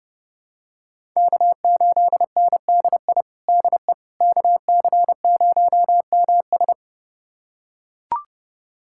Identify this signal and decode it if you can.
Morse code, heard as a keyed tone, about 30 wpm, 700 Hz. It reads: K8NDI DE KC0MS
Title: USA Callsign DE Callsign - 5 Characters - 30wpm